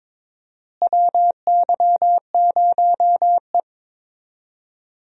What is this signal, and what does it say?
Morse code, heard as a keyed tone, about 22 wpm, 700 Hz. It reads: WY0E
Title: USA Call Signs - 4 Characters - Repeated 3x - 22wpm